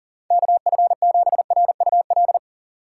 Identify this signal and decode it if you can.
Morse code, heard as a keyed tone, about 40 wpm, 700 Hz. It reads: KF7RUL
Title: USA Call Signs - 6 Characters - 2x Character Spacing - 40wpm